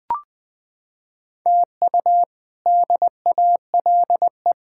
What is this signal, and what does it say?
Morse code, heard as a keyed tone, about 20 wpm, 700 Hz. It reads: TU DALE
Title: K1USN SST QSO Practice - 20wpm